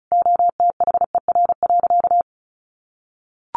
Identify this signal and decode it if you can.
Morse code, heard as a keyed tone, about 34 wpm, 700 Hz. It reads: OTHER.
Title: Sentences from Top 1200 Words 35wpm